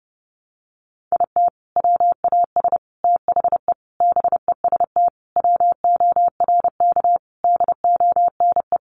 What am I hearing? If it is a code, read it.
Morse code, heard as a keyed tone, about 30 wpm, 700 Hz. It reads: IT WAS THE BEST WORK DONE
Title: Sentences from Top 1300 Words 30wpm